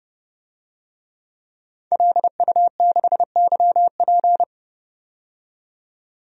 Morse code, heard as a keyed tone, about 30 wpm, 700 Hz. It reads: LU6YP
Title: DXCC-100 Call Signs - 5 Characters - 30wpm